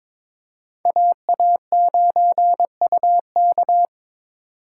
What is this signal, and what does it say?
Morse code, heard as a keyed tone, about 21 wpm, 700 Hz. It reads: AA9UK
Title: USA Call Signs - 5 Characters - 6x Character Spacing - 22wpm